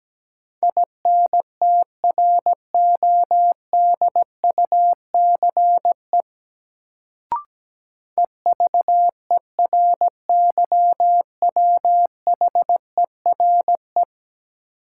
Morse code, heard as a keyed tone, about 17 wpm, 700 Hz. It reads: INTRODUCE EVERYWHERE
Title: Sentences from Top 1600 Words - New Word Drill - 17wpm